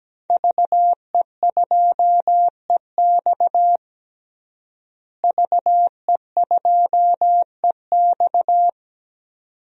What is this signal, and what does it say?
Morse code, heard as a keyed tone, about 17 wpm, 700 Hz. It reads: VE2EX VE2EX
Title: DXCC-100 Call Signs - 5 Characters - Repeated 2x - 17wpm